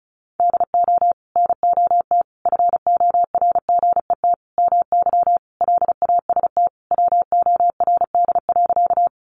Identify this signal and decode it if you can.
Morse code, heard as a keyed tone, about 34 wpm, 700 Hz. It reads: DO NOT FORGET MY LAST WORD.
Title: Sentences from Top 1200 Words 35wpm